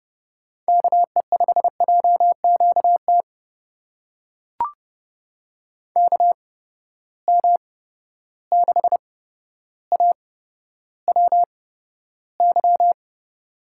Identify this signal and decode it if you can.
Morse code, heard as a keyed tone, about 30 wpm, spaced as for 13 wpm, 700 Hz. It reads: KE5JQT KM6AWY
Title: USA Call Signs - 6 Characters - 8x Character Spacing - 30wpm